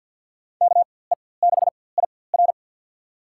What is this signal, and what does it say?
Morse code, heard as a keyed tone, about 53 wpm, spaced as for 24 wpm, 700 Hz. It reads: KE6IR
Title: USA Call Signs - 5 Characters - 4x Character Spacing - 50wpm